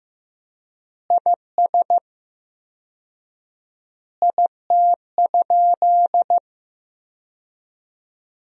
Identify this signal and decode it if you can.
Morse code, heard as a keyed tone, about 15 wpm, 700 Hz. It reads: IS IT?
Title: 4x Word-Spacing - Sentences from Top 200 Words - 15wpm